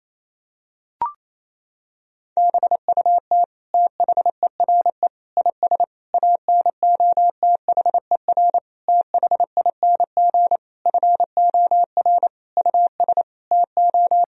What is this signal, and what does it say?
Morse code, heard as a keyed tone, about 27 wpm, 700 Hz. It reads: BUT THERE IS ANOTHER THING FOR US TO
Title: Sentences from Top 1000 Words - Review Part 4 - 28wpm